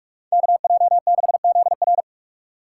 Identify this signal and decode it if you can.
Morse code, heard as a keyed tone, about 46 wpm, 700 Hz. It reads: KJ6ZR